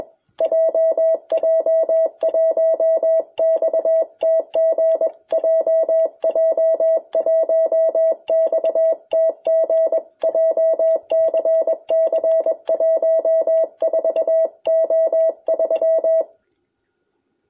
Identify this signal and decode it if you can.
Morse code, heard as a keyed tone, about 21 wpm, 650 Hz. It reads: JJ1XTG JJ1XTG JCC 14O3
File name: jh8cbh / 2021cw